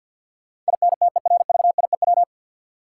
Morse code, heard as a keyed tone, about 53 wpm, 700 Hz. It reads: INTERVIEW
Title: Sentences from Top 1600 Words - New Word Drill Alt - 50wpm